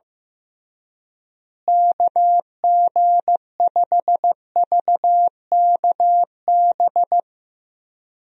Morse code, heard as a keyed tone, about 15 wpm, 700 Hz. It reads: KG5VKB
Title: USA Call Signs - 6 Characters - Repeated 3x - 15wpm